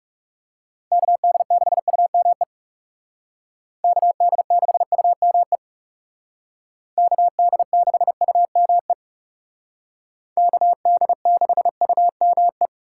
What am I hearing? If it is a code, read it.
Morse code, heard as a keyed tone, about 39 wpm, 700 Hz. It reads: KD6UME KD6UME KD6UME KD6UME